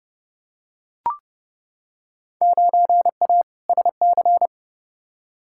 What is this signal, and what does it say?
Morse code, heard as a keyed tone, about 31 wpm, 700 Hz. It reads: 9A SC
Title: Field Day Exchanges Encore - 30wpm